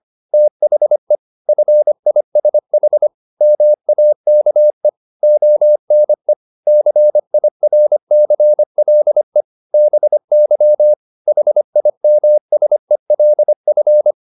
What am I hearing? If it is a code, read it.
Morse code, heard as a keyed tone, about 25 wpm, 600 Hz. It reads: THE FISH MAKE ONE CIRCLE BY HIMSELF